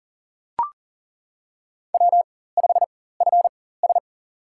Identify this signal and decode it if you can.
Morse code, heard as a keyed tone, about 40 wpm, spaced as for 19 wpm, 700 Hz. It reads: W5FS
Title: USA Call Signs - 4 Characters - 4x Character Spacing - 40wpm